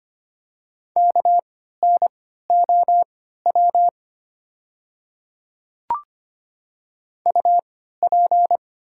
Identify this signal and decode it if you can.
Morse code, heard as a keyed tone, about 25 wpm, spaced as for 14 wpm, 700 Hz. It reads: KNOW UP
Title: 3x Character Spacing - Top 200 Words - 25wpm